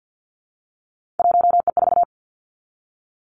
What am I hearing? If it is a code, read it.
Morse code, heard as a keyed tone, about 50 wpm, 700 Hz. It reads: 1E4
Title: Three Challenging Letter-Number Combinations 50wpm